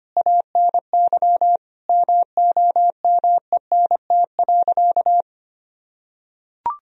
Morse code, heard as a keyed tone, about 25 wpm, 700 Hz. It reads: ANY MOMENT.
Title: Sentences from Top 1000 Words - Review Part 3 - 25wpm